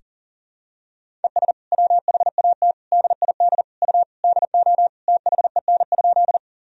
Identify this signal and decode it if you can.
Morse code, heard as a keyed tone, about 37 wpm, 700 Hz. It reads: ES WHAT DID U DO THEN?
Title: Sentences from Top 100 Words and Abbreviations 40wpm